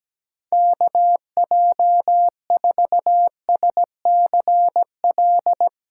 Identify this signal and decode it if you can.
Morse code, heard as a keyed tone, about 17 wpm, 700 Hz. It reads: KJ4SCL
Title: USA Call Signs - 6 Characters - 6x Character Spacing - 17wpm